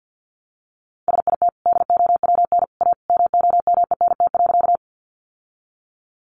Morse code, heard as a keyed tone, about 48 wpm, 700 Hz. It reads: SIT DOWN A MOMENT.
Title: Sentences from Top 1000 Words 50wpm